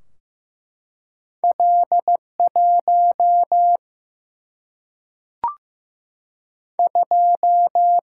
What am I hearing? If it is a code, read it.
Morse code, heard as a keyed tone, about 15 wpm, 700 Hz. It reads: L1 2
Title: Single-Letter Single-Number 15wpm